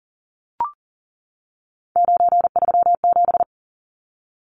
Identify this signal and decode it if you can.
Morse code, heard as a keyed tone, about 40 wpm, 700 Hz. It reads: 937
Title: Numbers - 3-Digit 40wpm